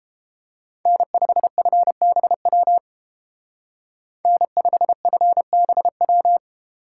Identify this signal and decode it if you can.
Morse code, heard as a keyed tone, about 29 wpm, 700 Hz. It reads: N5FBW N5FBW